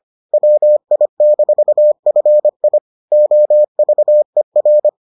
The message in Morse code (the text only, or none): WI-FI OVER